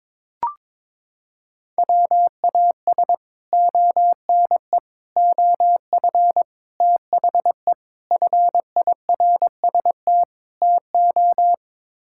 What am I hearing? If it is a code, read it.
Morse code, heard as a keyed tone, about 22 wpm, 700 Hz. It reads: WAS ONE OF THE FIRST TO
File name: practice / 22wpm / Sets of 6 Words - 22wpm